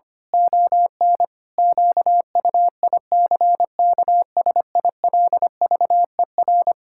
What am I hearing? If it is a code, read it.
Morse code, heard as a keyed tone, about 25 wpm, 700 Hz. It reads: ON QUICKSILVER
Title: Pangram Challenge v2 - 25wpm